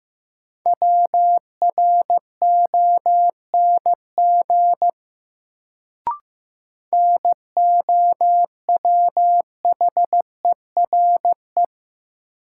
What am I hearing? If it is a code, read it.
Morse code, heard as a keyed tone, about 15 wpm, 700 Hz. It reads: WRONG NOWHERE
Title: Sets of 5 Words Encore - New Word Drill - 15wpm